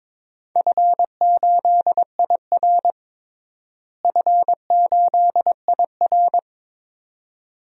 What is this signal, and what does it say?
Morse code, heard as a keyed tone, about 22 wpm, 700 Hz. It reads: F8IR F8IR
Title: DXCC-100 Call Signs - 4 Characters - Repeated 2x - 22wpm